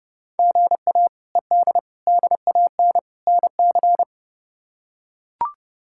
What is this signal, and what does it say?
Morse code, heard as a keyed tone, about 30 wpm, 700 Hz. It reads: GA ED DAN NC
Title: K1USN SST QSO Practice - 30wpm